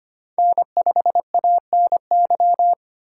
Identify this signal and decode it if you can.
Morse code, heard as a keyed tone, about 25 wpm, 700 Hz. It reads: N5ANY
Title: USA Call Signs - 3x Character Spacing - 25wpm